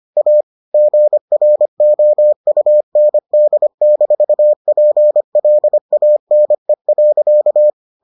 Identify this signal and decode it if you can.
Morse code, heard as a keyed tone, about 25 wpm, 600 Hz. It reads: A GROUND-PLANE.